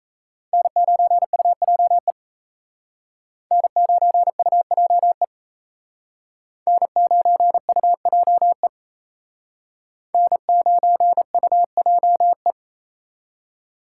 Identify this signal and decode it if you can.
Morse code, heard as a keyed tone, about 35 wpm, 700 Hz. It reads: N9UJE N9UJE N9UJE N9UJE